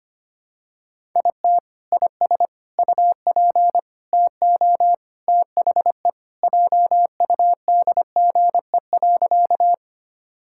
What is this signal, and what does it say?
Morse code, heard as a keyed tone, about 25 wpm, 700 Hz. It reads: IT IS UP TO THE JUDGE.
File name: Sentences with 3-Word Phrases - 25wpm